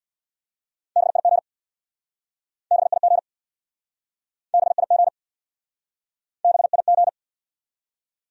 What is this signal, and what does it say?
Morse code, heard as a keyed tone, about 64 wpm, 700 Hz. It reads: BIG BIG BIG BIG